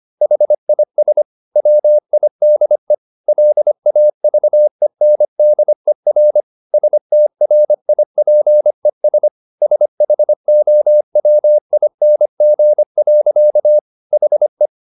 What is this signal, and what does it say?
Morse code, heard as a keyed tone, about 25 wpm, 600 Hz. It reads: HIS WIDE LAVENDER STRIPES SHOWING. HE
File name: the old man and the sea part 1 25wpm